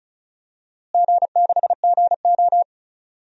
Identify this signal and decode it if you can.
Morse code, heard as a keyed tone, about 34 wpm, 700 Hz. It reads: G6GO
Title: DXCC-100 Call Signs - 4 Characters - Repeated 3x - 35wpm